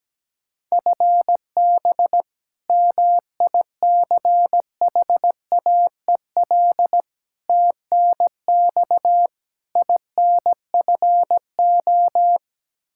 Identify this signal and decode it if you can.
Morse code, heard as a keyed tone, about 17 wpm, 700 Hz. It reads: FB MICHAEL TNX INFO
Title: Practice QSO - Exchange 3 - More Info - 17wpm